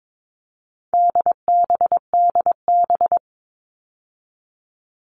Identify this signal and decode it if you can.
Morse code, heard as a keyed tone, about 22 wpm, 700 Hz. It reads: DBDB
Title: All Cut Numbers - 4-Digit 22wpm